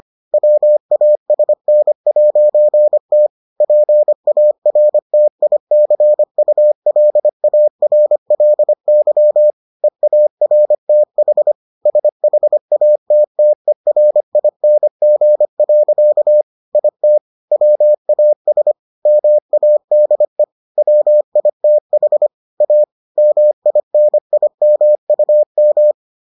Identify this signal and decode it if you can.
Morse code, heard as a keyed tone, about 25 wpm, 600 Hz. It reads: WASN'T PARTICULARLY EARTH SHATTERING. IT WAS MADE WITH A MINIMUM